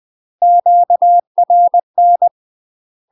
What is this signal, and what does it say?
Morse code, heard as a keyed tone, about 20 wpm, 700 Hz. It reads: QRN
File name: UK MASTER QSOs TRAINING / 20WPM / UK MASTER QSOs 0046